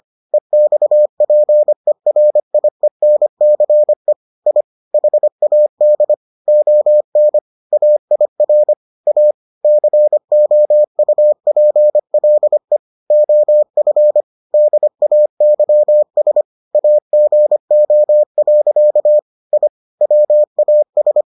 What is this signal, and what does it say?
Morse code, heard as a keyed tone, about 25 wpm, 600 Hz. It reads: EXPERIENCE I HAD ON AIR A COUPLE OF DAYS AGO. I WAS